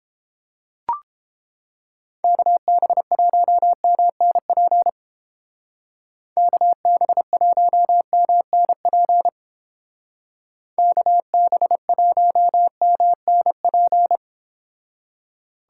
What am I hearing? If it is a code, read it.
Morse code, heard as a keyed tone, about 31 wpm, 700 Hz. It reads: KB1MNP KB1MNP KB1MNP